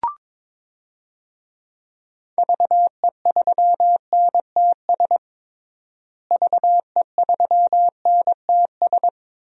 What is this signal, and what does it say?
Morse code, heard as a keyed tone, about 22 wpm, 700 Hz. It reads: VE3NTS VE3NTS
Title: DXCC-100 Call Signs - 6 Characters - Repeated 3x - 22wpm